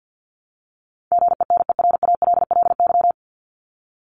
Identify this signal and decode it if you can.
Morse code, heard as a keyed tone, about 50 wpm, 700 Hz. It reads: GENERALLY
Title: LY Suffix - Top 100 Words with Base Words 50wpm